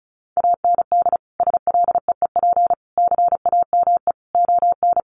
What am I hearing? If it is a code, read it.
Morse code, heard as a keyed tone, about 34 wpm, 700 Hz. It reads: AND SLEEP CAME ON
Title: Sentences from Top 1500 Words 35wpm